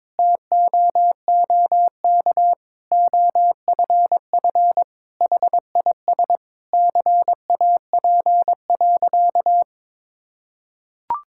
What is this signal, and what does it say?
Morse code, heard as a keyed tone, about 22 wpm, 700 Hz. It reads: TOOK OFF HIS CAP.
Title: Sentences from Top 1600 Words 22wpm